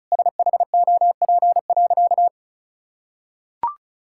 Morse code, heard as a keyed tone, about 34 wpm, 700 Hz. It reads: SHOP.
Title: Sentences from Top 1600 Words 35wpm